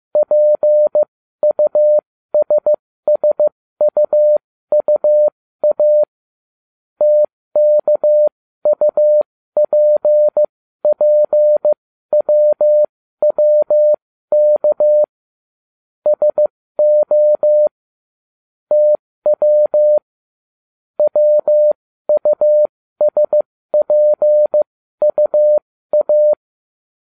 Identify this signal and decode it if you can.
Morse code, heard as a keyed tone, about 15 wpm, spaced as for 13 wpm, 600 Hz. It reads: PUSSUUA TKUPPWWK SO TW WUSPUA